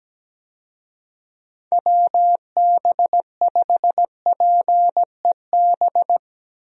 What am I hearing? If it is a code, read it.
Morse code, heard as a keyed tone, about 17 wpm, 700 Hz. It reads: WB5PEB
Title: USA Call Signs - 6 Characters - 17wpm